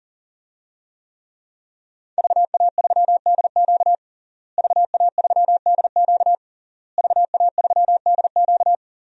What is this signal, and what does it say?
Morse code, heard as a keyed tone, about 40 wpm, 700 Hz. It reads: VA3DQ VA3DQ VA3DQ
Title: DXCC-100 Call Signs - 5 Characters - Repeated 3x - 40wpm